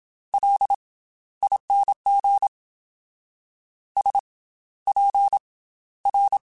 Morse code, heard as a keyed tone, about 27 wpm, 800 Hz. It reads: L ING S P R
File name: Special INGs / Special INGs-mp3s / Slow ing @25